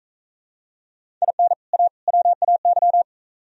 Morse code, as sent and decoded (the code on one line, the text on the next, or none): .. -. / .- / .-- .- -.--
IN A WAY